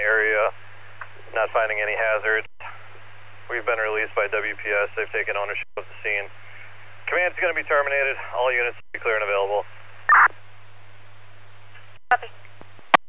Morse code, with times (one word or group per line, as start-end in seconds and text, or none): none